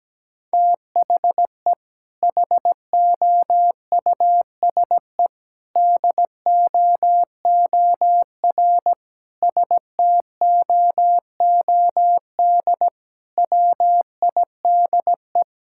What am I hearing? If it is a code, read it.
Morse code, heard as a keyed tone, about 17 wpm, 700 Hz. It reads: THE HOUSE DOOR STOOD WIDE
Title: Sentences from Top 1000 Words - Review Part 6 - 17wpm